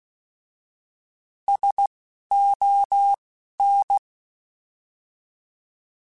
Morse code, heard as a keyed tone, about 16 wpm, spaced as for 11 wpm, 800 Hz. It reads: SON
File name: Alphabetical / Alphabetical mp3s / Ss @10